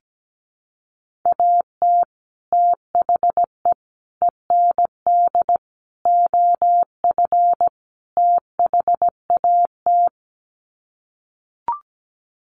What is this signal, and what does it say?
Morse code, heard as a keyed tone, about 17 wpm, 700 Hz. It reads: AT THE END OF THAT
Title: Sets of 5 Words Encore2 17wpm